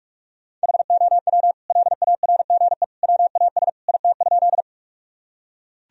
Morse code, heard as a keyed tone, about 41 wpm, 700 Hz. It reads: HOW LARGE WAS IT?
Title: Sentences from Top 1000 Words - Review Part 1 - 45wpm